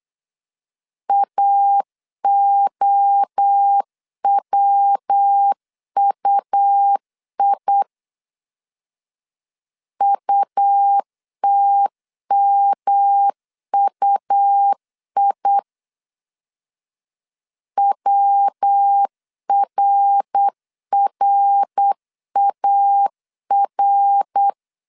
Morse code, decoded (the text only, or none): AOWUI UTMUI WRRAR